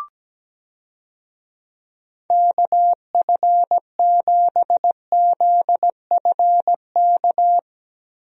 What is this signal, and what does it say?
Morse code, heard as a keyed tone, about 17 wpm, 700 Hz. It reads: KF7ZFK
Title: USA Call Signs - 6 Characters - Repeated 3x - 17wpm